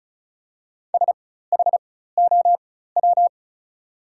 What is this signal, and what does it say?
Morse code, heard as a keyed tone, about 36 wpm, spaced as for 16 wpm, 700 Hz. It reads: SHOW
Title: 4x Character Spacing - Top 200 Words - 35wpm